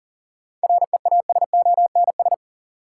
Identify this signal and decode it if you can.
Morse code, heard as a keyed tone, about 39 wpm, 700 Hz. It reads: REASONS
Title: Sets of 4 Words - New Word Drill - 40wpm